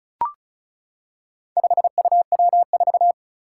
Morse code, heard as a keyed tone, about 36 wpm, 700 Hz. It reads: 5UW4
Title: Four Challenging Letter-Number Combinations 35wpm